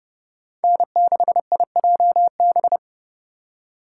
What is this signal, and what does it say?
Morse code, heard as a keyed tone, about 29 wpm, 700 Hz. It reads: N6IJB